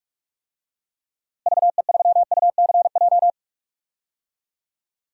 Morse code, heard as a keyed tone, about 43 wpm, 700 Hz. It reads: VE3UCJ
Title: DXCC-100 Call Signs - 6 Characters - Repeated 3x - 45wpm